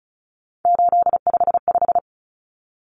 Morse code, heard as a keyed tone, about 34 wpm, 700 Hz. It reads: 855
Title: Common Cut Numbers - 3-Digit 35wpm